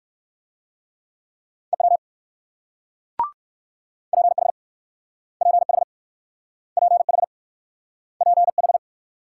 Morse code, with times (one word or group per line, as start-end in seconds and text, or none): 1.7-2.0 EW
4.1-4.5 PH
5.4-5.8 PH
6.8-7.2 PH
8.2-8.8 PH